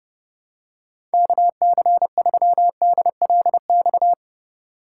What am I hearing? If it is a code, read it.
Morse code, heard as a keyed tone, about 30 wpm, 700 Hz. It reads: KC3DLX
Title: USA Call Signs - 6 Characters - 3x Character Spacing - 30wpm